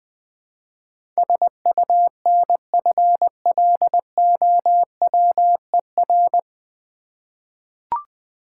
Morse code, heard as a keyed tone, about 20 wpm, 690 Hz. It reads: SUNFLOWER
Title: Difficult Words with Hints - 20wpm